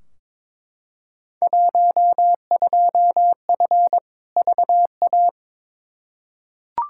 Morse code, heard as a keyed tone, about 22 wpm, 700 Hz. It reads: 12F VA
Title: Field Day Exchanges Encore - 22wpm